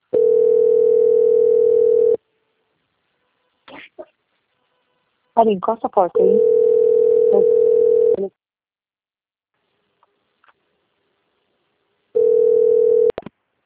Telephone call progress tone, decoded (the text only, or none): ringback tone